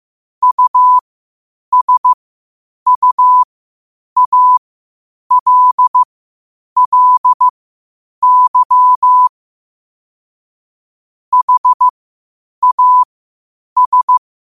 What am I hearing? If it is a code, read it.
Morse code, heard as a keyed tone, about 15 wpm, spaced as for 8 wpm, 1000 Hz. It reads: USUALLY HAS